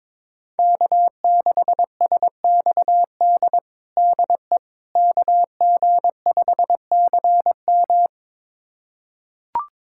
Morse code, heard as a keyed tone, about 22 wpm, 700 Hz. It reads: K6SXD DE KG5CM
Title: USA Callsign DE Callsign - 5 Characters - 22wpm